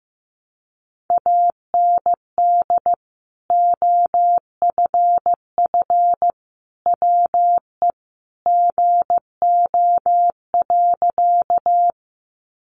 Morse code, heard as a keyed tone, about 15 wpm, 700 Hz. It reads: AND OFF WE GO.
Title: Sentences from Top 200 Words 15wpm